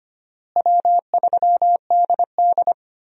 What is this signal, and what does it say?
Morse code, heard as a keyed tone, about 25 wpm, 700 Hz. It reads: W3DD
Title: USA Call Signs - 4 Characters - Repeated 2x - 25wpm